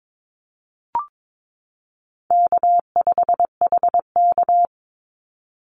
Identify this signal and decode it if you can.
Morse code, heard as a keyed tone, about 22 wpm, 700 Hz. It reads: K5HK
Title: Call Signs Repeated 2x - 22wpm